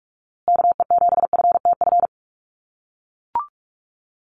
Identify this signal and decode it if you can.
Morse code, heard as a keyed tone, about 44 wpm, 700 Hz. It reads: KE7FTF
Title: USA Call Signs Repeated 2x 45wpm